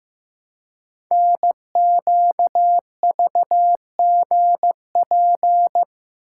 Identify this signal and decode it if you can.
Morse code, heard as a keyed tone, about 15 wpm, 700 Hz. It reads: NQVGP
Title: Code Groups - 5 Letters - 15wpm